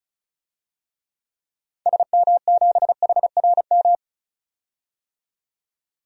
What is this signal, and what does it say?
Morse code, heard as a keyed tone, about 36 wpm, 690 Hz. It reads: SM7HRM
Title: DXCC-100 Call Signs - 6 Characters - 35wpm